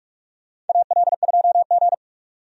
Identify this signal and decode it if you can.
Morse code, heard as a keyed tone, about 43 wpm, 700 Hz. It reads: AL2G
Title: USA Call Signs - 4 Characters - 45wpm